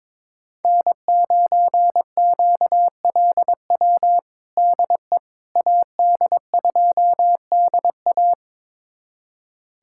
Morse code, heard as a keyed tone, about 22 wpm, 700 Hz. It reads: N9QLW DE AD2DA
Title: USA Callsign DE Callsign - 5 Characters - 22wpm